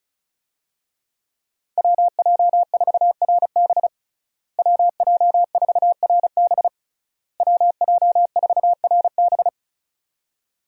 Morse code, heard as a keyed tone, about 36 wpm, 700 Hz. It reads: WJ4RB WJ4RB WJ4RB
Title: USA Call Signs - 5 Characters - Repeated 3x - 35wpm